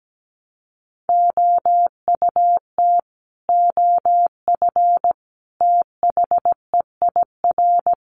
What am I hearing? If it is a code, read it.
Morse code, heard as a keyed tone, about 17 wpm, 700 Hz. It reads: OUT OF THEIR
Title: Sets of 5 Words Encore2 17wpm